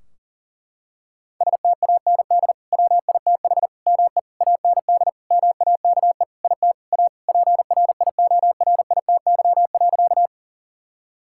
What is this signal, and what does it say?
Morse code, heard as a keyed tone, about 40 wpm, 700 Hz. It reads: STAND WITH ME AND MAKE IT A PRIORITY.
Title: Sentences from Top 1600 Words 40wpm